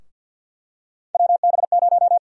Morse code, heard as a keyed tone, about 52 wpm, 700 Hz. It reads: WB0
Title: USA Call Sign Prefixes - Repeated 2x - 50wpm